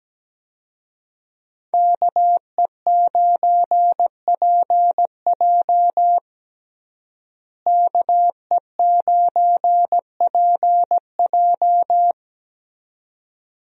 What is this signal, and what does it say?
Morse code, heard as a keyed tone, about 17 wpm, 700 Hz. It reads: KE9PJ KE9PJ
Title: USA Call Signs - 5 Characters - Repeated 2x - 17wpm